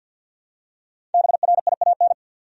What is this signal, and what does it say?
Morse code, heard as a keyed tone, about 50 wpm, 700 Hz. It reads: BRIAN
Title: USA Common Names - Max 5 Letters - 50wpm